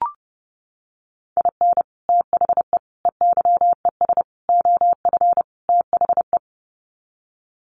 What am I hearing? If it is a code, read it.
Morse code, heard as a keyed tone, about 30 wpm, 700 Hz. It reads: IN THE EYES OF THE
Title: Sets of 5 Words 30wpm